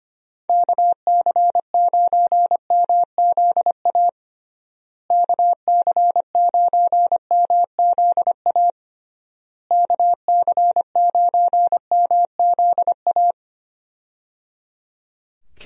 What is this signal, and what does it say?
Morse code, heard as a keyed tone, about 25 wpm, 700 Hz. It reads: KC9MZA KC9MZA KC9MZA
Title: USA Call Signs - 6 Characters - Repeated 3x - 25wpm